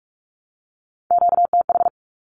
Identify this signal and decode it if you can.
Morse code, heard as a keyed tone, about 44 wpm, 700 Hz. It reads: QTH
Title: Q-Codes Repeated 3x 45wpm